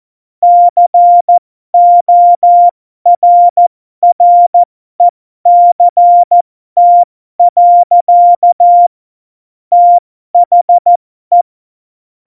Morse code, heard as a keyed tone, about 14 wpm, spaced as for 12 wpm, 700 Hz. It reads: CORRECT. THE
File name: Morsekurs / Witze